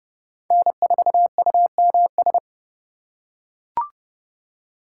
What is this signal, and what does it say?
Morse code, heard as a keyed tone, about 30 wpm, 700 Hz. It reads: N4UMS
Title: USA Call Signs - 5 Characters - 2x Character Spacing - 30wpm